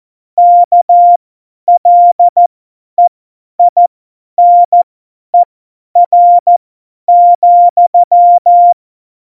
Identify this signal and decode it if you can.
Morse code, heard as a keyed tone, about 14 wpm, spaced as for 10 wpm, 700 Hz. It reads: KLEINER,
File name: Morsekurs / Witze / Witz 05. Dezember 2017, 50 F70